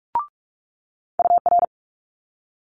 Morse code, heard as a keyed tone, about 46 wpm, 700 Hz. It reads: UR
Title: Multi-letter Phonemes Quick Fire 45wpm